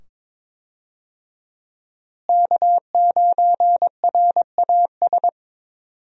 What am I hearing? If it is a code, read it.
Morse code, heard as a keyed tone, about 22 wpm, 700 Hz. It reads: K9RAS
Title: USA Call Signs - 5 Characters - Repeated 2x - 22wpm